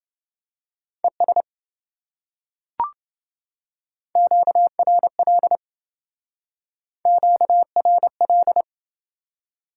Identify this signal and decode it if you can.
Morse code, heard as a keyed tone, about 27 wpm, 700 Hz. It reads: ES QRL QRL